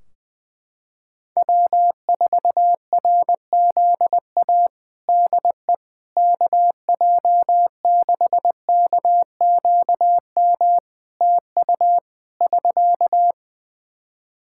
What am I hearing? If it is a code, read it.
Morse code, heard as a keyed tone, about 20 wpm, 700 Hz. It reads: W4RZA DE KJ6KQM TU <SK>
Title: Practice QSO - Exchange 4 - Ending - 20wpm